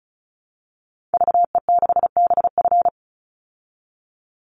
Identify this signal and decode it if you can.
Morse code, heard as a keyed tone, about 34 wpm, 700 Hz. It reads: VE6BF
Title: Call Signs Repeated 2x - 35wpm